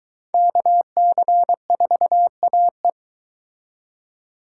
Morse code, heard as a keyed tone, about 23 wpm, 700 Hz. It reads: KC4AE